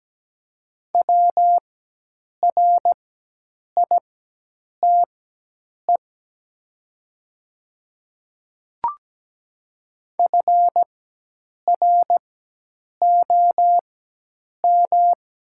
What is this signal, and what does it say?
Morse code, heard as a keyed tone, about 17 wpm, spaced as for 8 wpm, 700 Hz. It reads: WRITE FROM
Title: 4x Character Spacing - Top 100 Words - 17wpm